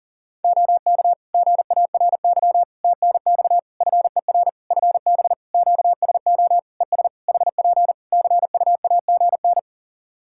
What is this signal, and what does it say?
Morse code, heard as a keyed tone, about 40 wpm, 700 Hz. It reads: OK GARY TNX FER FB QSO ES HP CUAGN
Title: Practice QSO - Exchange 4 - Ending - 40wpm